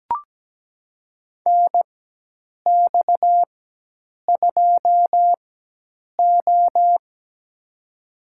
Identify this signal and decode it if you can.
Morse code, heard as a keyed tone, about 17 wpm, spaced as for 8 wpm, 700 Hz. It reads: NX2O
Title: USA Call Signs - 4 Characters - 4x Character Spacing - 17wpm